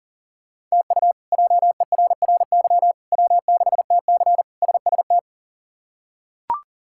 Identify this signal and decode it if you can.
Morse code, heard as a keyed tone, about 37 wpm, 700 Hz. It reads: TU JERRY W6TC SST
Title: K1USN SST QSO Practice - 40wpm